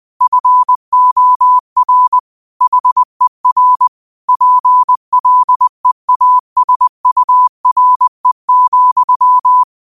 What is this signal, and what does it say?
Morse code, heard as a keyed tone, about 20 wpm, 1000 Hz. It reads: FOR HER PLEASURE,
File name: qotdfiles / 20wpm